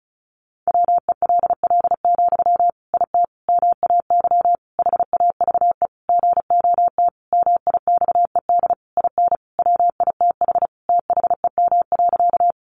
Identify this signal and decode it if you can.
Morse code, heard as a keyed tone, about 34 wpm, 700 Hz. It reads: WELL, IT MAY HAVE GOT MIXED IN WITH THEM.
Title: Sentences from Top 1500 Words 35wpm